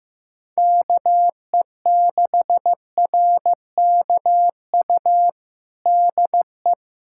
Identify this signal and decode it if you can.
Morse code, heard as a keyed tone, about 16 wpm, 700 Hz. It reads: KE6RKU DE
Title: USA Callsign DE Callsign 15wpm